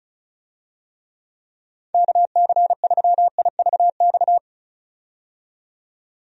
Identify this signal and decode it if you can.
Morse code, heard as a keyed tone, about 36 wpm, 700 Hz. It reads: KC3IVX
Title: USA Call Signs - 6 Characters - 35wpm